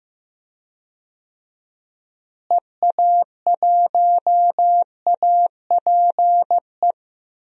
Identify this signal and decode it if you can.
Morse code, heard as a keyed tone, about 15 wpm, 700 Hz. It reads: EA1APE
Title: DXCC-100 Call Signs - 6 Characters - Repeated 2x - 15wpm